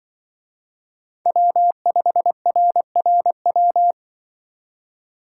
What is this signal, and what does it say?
Morse code, heard as a keyed tone, about 24 wpm, 700 Hz. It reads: W5RRW
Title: Speed Racing - USA Call Signs 20wpm